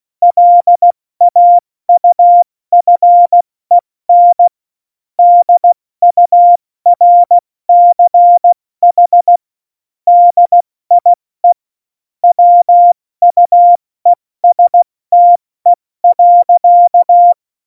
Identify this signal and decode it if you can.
Morse code, heard as a keyed tone, about 16 wpm, 700 Hz. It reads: LAUFEN DURCH DIE WUESTE.
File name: Morsekurs / Witze / Witz 24.Juli 2018, 70 F80